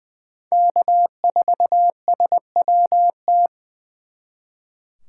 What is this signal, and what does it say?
Morse code, heard as a keyed tone, about 24 wpm, 700 Hz. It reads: K4SWT
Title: Speed Racing - USA 1x3 Call Signs 20wpm